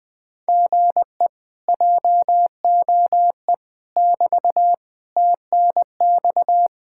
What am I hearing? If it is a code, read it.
Morse code, heard as a keyed tone, about 24 wpm, 700 Hz. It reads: GE JOE <BT> TNX
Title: Speed-Racing Common QSO Phrases Encore - 20wpm